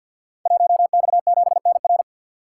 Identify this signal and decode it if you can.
Morse code, heard as a keyed tone, about 52 wpm, 700 Hz. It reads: 1X7NR
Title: Code Groups - 5 Letter:Numbers - 50wpm